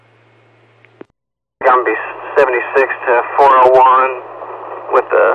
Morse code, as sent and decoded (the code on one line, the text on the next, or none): none